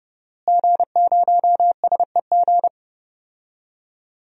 Morse code, heard as a keyed tone, about 30 wpm, 700 Hz. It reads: G0SEG
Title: DXCC-100 Call Signs - 5 Characters - Repeated 3x - 30wpm